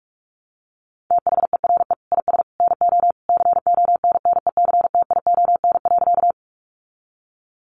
Morse code, heard as a keyed tone, about 44 wpm, 700 Hz. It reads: THERE IS NO CONNECTION.
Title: Sentences from Top 1300 Words 45wpm